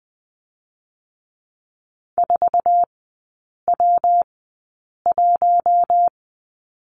Morse code, heard as a keyed tone, about 20 wpm, 700 Hz. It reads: 4 W 1
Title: 114 Lesson 19e - Letter w - Three Characters 20wpm